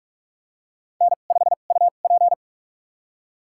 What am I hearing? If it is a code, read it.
Morse code, heard as a keyed tone, about 45 wpm, spaced as for 32 wpm, 700 Hz. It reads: N5UP
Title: USA Call Signs - 2x Character Spacing - 45wpm